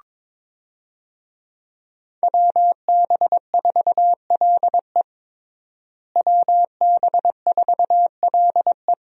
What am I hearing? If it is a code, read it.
Morse code, heard as a keyed tone, about 22 wpm, 700 Hz. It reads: WB4LE WB4LE
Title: USA Call Signs - 5 Characters - Repeated 2x - 22wpm